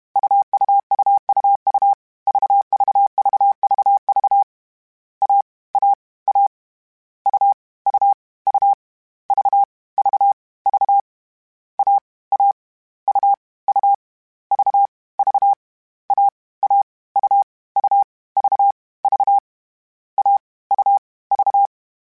Morse code, heard as a keyed tone, about 32 wpm, 800 Hz. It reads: UUUUU VVVVV A A A U U U V V V A A U U V V A A U U V V A U V